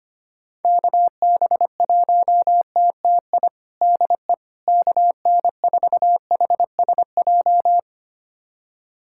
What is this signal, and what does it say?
Morse code, heard as a keyed tone, about 25 wpm, 700 Hz. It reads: KB1TTI DE KN4HSJ
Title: USA Callsign DE Callsign 25wpm